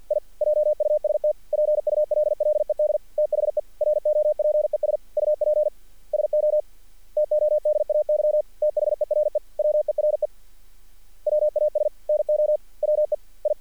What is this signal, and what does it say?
Morse code, heard as a keyed tone, about 48 wpm, 590 Hz. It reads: I JUST PULLED THE ROPES UP SO TODAY THERE WERE WAS NO WE I
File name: archived / CRSnet